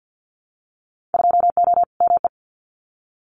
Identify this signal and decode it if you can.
Morse code, heard as a keyed tone, about 50 wpm, 700 Hz. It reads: 2O ME
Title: Winter Field Day Exchanges 50wpm